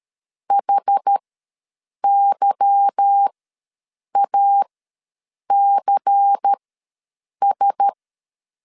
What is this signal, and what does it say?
Morse code, heard as a keyed tone, about 13 wpm, spaced as for 7 wpm, 800 Hz. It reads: HYACS
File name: LEZIONI / 8 YJ45 / 8 YJ45 ws 12wpm 3146